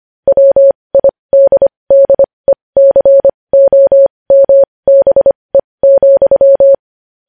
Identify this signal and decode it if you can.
Morse code, heard as a keyed tone, about 25 wpm, spaced as for 20 wpm, 550 Hz. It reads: WIDDECOMBE,
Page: news headlines in morse code at 20 wpm